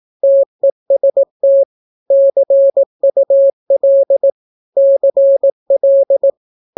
Morse code, heard as a keyed tone, about 18 wpm, 550 Hz. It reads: TEST CUL CL
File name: naukapodstaw / skroty / skroty